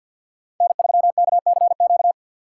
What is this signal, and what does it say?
Morse code, heard as a keyed tone, about 46 wpm, 700 Hz. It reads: N3KCQ
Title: USA Call Signs - 5 Characters - 3x Character Spacing - 50wpm